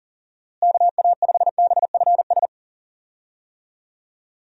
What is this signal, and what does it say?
Morse code, heard as a keyed tone, about 40 wpm, 700 Hz. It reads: KA5BFS
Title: USA Call Signs - 6 Characters - Repeated 2x - 40wpm